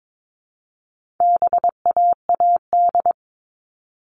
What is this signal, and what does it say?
Morse code, heard as a keyed tone, about 22 wpm, 700 Hz. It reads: BAAD
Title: All Cut Numbers - 4-Digit 22wpm